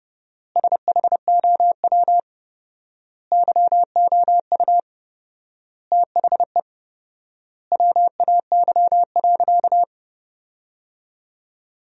Morse code, heard as a keyed tone, about 30 wpm, 700 Hz. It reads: SHOW YOU THE WAY.
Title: 4x Word-Spacing - Sentences from Top 300 Words - 30wpm